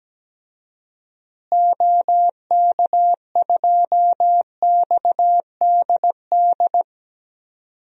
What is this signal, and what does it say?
Morse code, heard as a keyed tone, about 17 wpm, 700 Hz. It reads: OK2XDD
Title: DXCC-100 Call Signs - 6 Characters - Repeated 2x - 17wpm